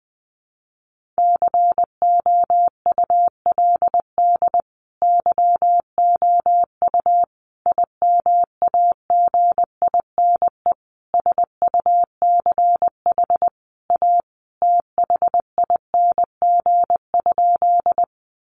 Morse code, heard as a keyed tone, about 19 wpm, 700 Hz. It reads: COULD YOU IMAGINE SUCH A THING?
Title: Sentences from Top 1400 Words 20wpm